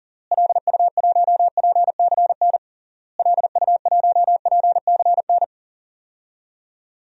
Morse code, heard as a keyed tone, about 37 wpm, 700 Hz. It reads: LU1PCN LU1PCN
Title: DXCC-100 Call Signs - 6 Characters - Repeated 3x - 40wpm